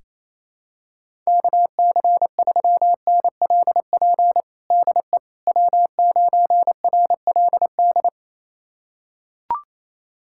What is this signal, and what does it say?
Morse code, heard as a keyed tone, about 28 wpm, 700 Hz. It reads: KC3NLP DE W9RLD
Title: USA Callsign DE Callsign 28wpm